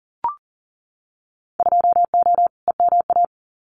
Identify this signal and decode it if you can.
Morse code, heard as a keyed tone, about 40 wpm, 700 Hz. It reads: 2O EMA
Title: Winter Field Day Exchanges 40wpm